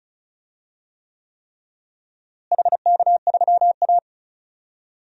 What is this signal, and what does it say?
Morse code, heard as a keyed tone, about 36 wpm, 700 Hz. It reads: HK3A
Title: DXCC-100 Call Signs - 4 Characters - Repeated 3x - 35wpm